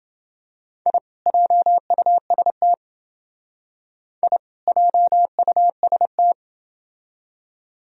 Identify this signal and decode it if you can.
Morse code, heard as a keyed tone, about 28 wpm, 700 Hz. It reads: I JUST I JUST